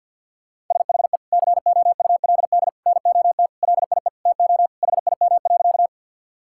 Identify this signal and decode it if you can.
Morse code, heard as a keyed tone, about 50 wpm, 700 Hz. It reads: SHE COULD NOT LIE TO HIM.
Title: Sentences from Top 1600 Words 50wpm